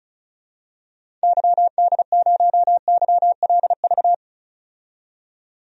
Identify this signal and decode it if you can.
Morse code, heard as a keyed tone, about 36 wpm, 700 Hz. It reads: YD0YLV
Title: DXCC-100 Call Signs - 6 Characters - 35wpm